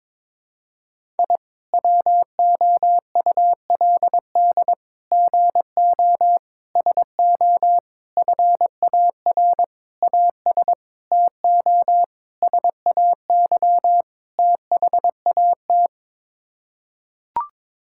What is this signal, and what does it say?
Morse code, heard as a keyed tone, about 22 wpm, 700 Hz. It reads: I WOULD GO SO FAR AS TO SAY THAT
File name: Sets of 9 Words - 22wpm